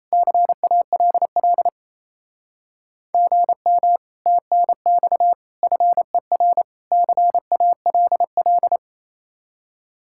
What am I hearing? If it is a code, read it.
Morse code, heard as a keyed tone, about 34 wpm, 700 Hz. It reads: CALL GM TNX FER CALL
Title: Speed-Racing Common QSO Phrases - 28wpm